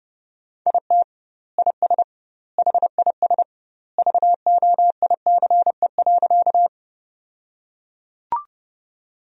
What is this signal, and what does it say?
Morse code, heard as a keyed tone, about 30 wpm, 700 Hz. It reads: IT IS HIS VOICE.
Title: 2x Word-Spacing - Sentences from Top 500 Words - 30wpm